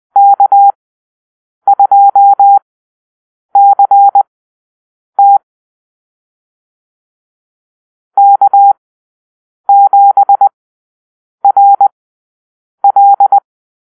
Morse code, heard as a keyed tone, about 20 wpm, spaced as for 7 wpm, 800 Hz. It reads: K2CT K7RL
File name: arrl / Call-Sign Drill-10